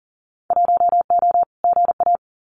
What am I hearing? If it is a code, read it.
Morse code, heard as a keyed tone, about 40 wpm, 700 Hz. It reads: 1O GA
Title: Winter Field Day Exchanges 40wpm